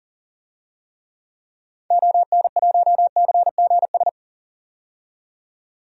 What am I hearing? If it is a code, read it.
Morse code, heard as a keyed tone, about 40 wpm, 700 Hz. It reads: ON1CGS
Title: DXCC-100 Call Signs - 6 Characters - 40wpm